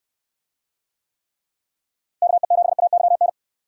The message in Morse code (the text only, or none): KE7AYN